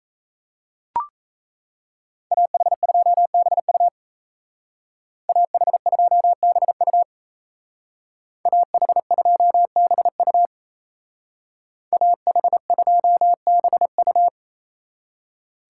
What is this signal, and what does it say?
Morse code, heard as a keyed tone, about 36 wpm, 700 Hz. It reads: AH2BU AH2BU AH2BU AH2BU